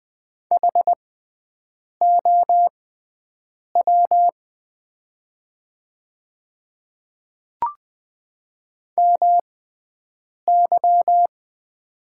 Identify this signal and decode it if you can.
Morse code, heard as a keyed tone, about 20 wpm, spaced as for 7 wpm, 700 Hz. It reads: HOW MY
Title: 6x Character Spacing - Top 100 Words - 20wpm